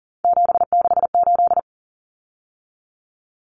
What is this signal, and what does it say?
Morse code, heard as a keyed tone, about 40 wpm, 700 Hz. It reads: 768
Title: Common Cut Numbers - 3-Digit 40wpm